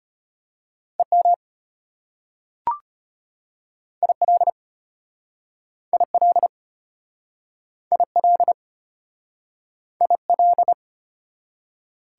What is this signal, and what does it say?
Morse code, heard as a keyed tone, about 34 wpm, 700 Hz. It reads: EM IL IL IL IL